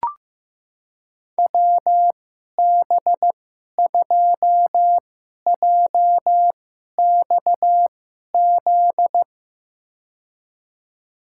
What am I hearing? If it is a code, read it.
Morse code, heard as a keyed tone, about 15 wpm, spaced as for 11 wpm, 700 Hz. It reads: WB2JXZ
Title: USA Call Signs - 6 Characters - 2x Character Spacing - 15wpm